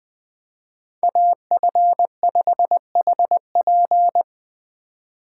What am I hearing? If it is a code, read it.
Morse code, heard as a keyed tone, about 20 wpm, 700 Hz. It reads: AF5HP
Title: Speed Racing - USA Call Signs 17wpm